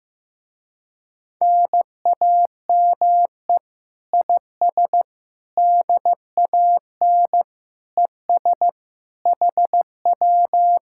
Morse code, heard as a keyed tone, about 15 wpm, 700 Hz. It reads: NAME IS DAN ES HW
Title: Speed-Racing Common QSO Phrases Encore - 15wpm